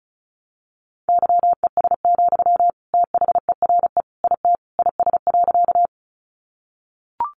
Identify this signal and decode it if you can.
Morse code, heard as a keyed tone, about 34 wpm, 700 Hz. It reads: YES, THERE IT IS.
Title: Sentences from Top 500 Words 35wpm